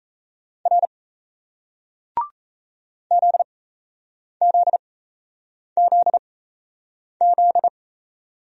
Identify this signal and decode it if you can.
Morse code, heard as a keyed tone, about 38 wpm, spaced as for 8 wpm, 700 Hz. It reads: R ZZZZ